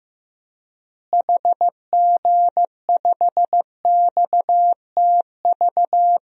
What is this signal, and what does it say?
Morse code, heard as a keyed tone, about 16 wpm, 700 Hz. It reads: HG5XTV
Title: DXCC-100 Call Signs - 6 Characters - Repeated 2x - 15wpm